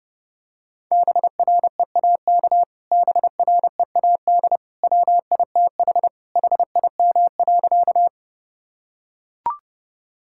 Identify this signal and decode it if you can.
Morse code, heard as a keyed tone, about 30 wpm, 700 Hz. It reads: BREAK BREAD WITH HIM.
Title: Sentences from Top 1600 Words 30wpm